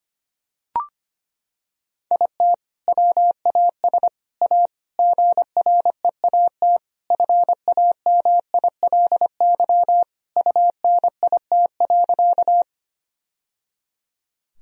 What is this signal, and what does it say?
Morse code, heard as a keyed tone, about 25 wpm, 700 Hz. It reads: IT WAS A GREAT FAMILY UNIT.
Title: Sentences from Top 1000 Words - Review Part 2 - 25wpm